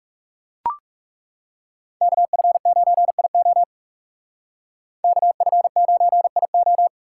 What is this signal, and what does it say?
Morse code, heard as a keyed tone, about 38 wpm, 700 Hz. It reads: KF9IO KF9IO